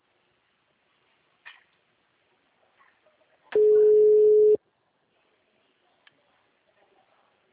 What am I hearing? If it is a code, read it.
Ringback tone (Europe, 425 Hz).